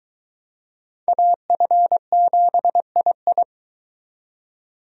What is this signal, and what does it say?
Morse code, heard as a keyed tone, about 23 wpm, 700 Hz. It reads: AF7II